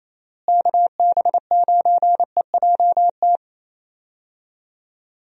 Morse code, heard as a keyed tone, about 28 wpm, 700 Hz. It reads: KB9EJT
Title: USA Call Signs - 6 Characters - Repeated 3x - 28wpm